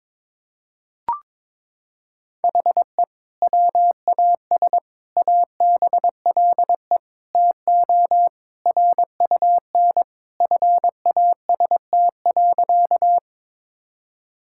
Morse code, heard as a keyed tone, about 22 wpm, 700 Hz. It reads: HE WAS ABLE TO RUN FAST.
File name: practice / 22wpm / Sentences with 2-Word Phrases - 22wpm